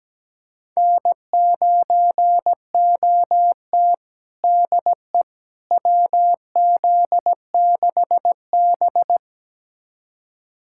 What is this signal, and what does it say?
Morse code, heard as a keyed tone, about 17 wpm, 700 Hz. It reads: N9OT DE WZ6B
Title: USA Callsign DE Callsign - 4 Characters - 17wpm